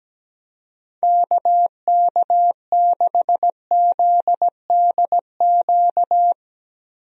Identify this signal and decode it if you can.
Morse code, heard as a keyed tone, about 17 wpm, 700 Hz. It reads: KK6ZDQ
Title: USA Call Signs - 6 Characters - 17wpm